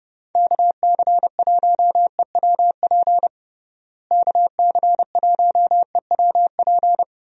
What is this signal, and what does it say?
Morse code, heard as a keyed tone, about 30 wpm, 700 Hz. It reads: KC1EWP KC1EWP
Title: USA Call Signs - 6 Characters - Repeated 2x - 30wpm